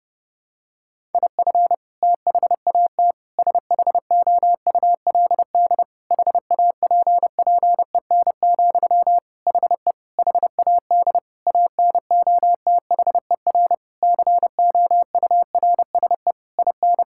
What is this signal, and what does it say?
Morse code, heard as a keyed tone, about 30 wpm, 700 Hz. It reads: IF THAT SHOULD HAPPEN, HE HAD ANOTHER COURSE IN
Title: Sentences from Top 1000 Words - Review Part 6 - 30wpm